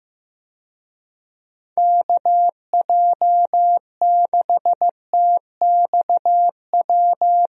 Morse code, heard as a keyed tone, about 15 wpm, 700 Hz. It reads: KJ6TXW
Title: USA Call Signs - 6 Characters - 15wpm